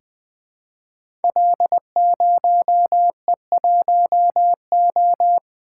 Morse code, heard as a keyed tone, about 20 wpm, 700 Hz. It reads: L0E1O
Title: Code Groups - 5 Letter:Numbers - 20wpm